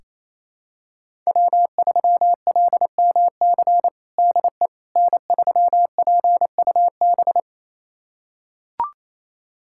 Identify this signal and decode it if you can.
Morse code, heard as a keyed tone, about 28 wpm, 700 Hz. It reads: W3LMC DE N3PUB
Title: USA Callsign DE Callsign - 5 Characters - 28wpm